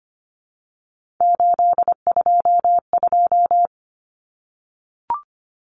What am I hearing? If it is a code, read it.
Morse code, heard as a keyed tone, about 25 wpm, 700 Hz. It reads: 822
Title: Common Cut Numbers - 3-Digit 25wpm